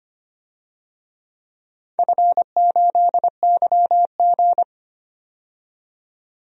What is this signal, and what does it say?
Morse code, heard as a keyed tone, about 25 wpm, 700 Hz. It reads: F8YG
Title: DXCC-100 Call Signs - 4 Characters - 25wpm